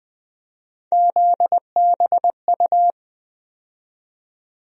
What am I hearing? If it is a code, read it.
Morse code, heard as a keyed tone, about 20 wpm, 700 Hz. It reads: ZBU
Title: USA Call Sign Postfixes - 3 Letters 20wpm